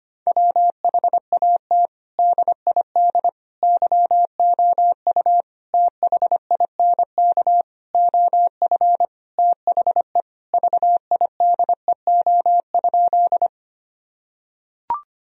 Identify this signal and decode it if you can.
Morse code, heard as a keyed tone, about 25 wpm, 700 Hz. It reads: WHAT DID YOU THINK OF THE VIDEO?
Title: Sentences from Top 1000 Words - Review Part 4 - 25wpm